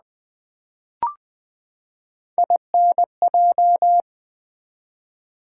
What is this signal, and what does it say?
Morse code, heard as a keyed tone, about 20 wpm, 700 Hz. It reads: INJ
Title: Build A Word - 3 Letters At A Time - 9-Letter Words - 20wpm